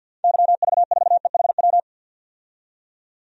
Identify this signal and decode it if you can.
Morse code, heard as a keyed tone, about 51 wpm, 700 Hz. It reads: YV4EHW
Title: DXCC-100 Call Signs - 6 Characters - 50wpm